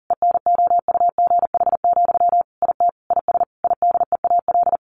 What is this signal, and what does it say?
Morse code, heard as a keyed tone, about 40 wpm, 700 Hz. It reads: ENOUGH, IT IS IDEAL
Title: Sentences from Top 1400 Words 40wpm